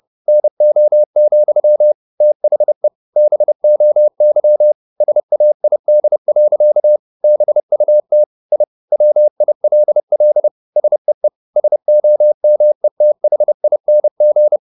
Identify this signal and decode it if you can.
Morse code, heard as a keyed tone, about 30 wpm, 600 Hz. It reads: NO, THE BOY SAID. BUT I WILL SEE SOMETHING
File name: the old man and the sea / the old man and the sea 1 30wpm